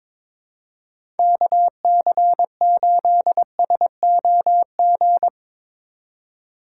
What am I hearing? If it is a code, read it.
Morse code, heard as a keyed tone, about 21 wpm, 700 Hz. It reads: KC8SOG
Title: USA Call Signs - 6 Characters - 2x Character Spacing - 22wpm